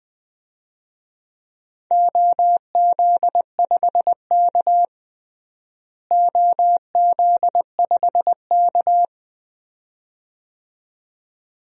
Morse code, heard as a keyed tone, about 20 wpm, 700 Hz. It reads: OZ5K OZ5K
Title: DXCC-100 Call Signs - 4 Characters - Repeated 2x - 20wpm